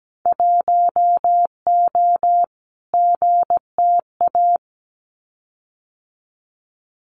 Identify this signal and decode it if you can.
Morse code, heard as a keyed tone, about 17 wpm, 700 Hz. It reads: 1O GTA
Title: Winter Field Day Exchanges 17wpm